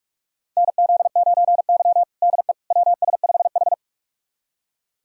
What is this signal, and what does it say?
Morse code, heard as a keyed tone, about 46 wpm, 700 Hz. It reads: NZ9Y DE WS5H
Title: USA Callsign DE Callsign - 4 Characters - 45wpm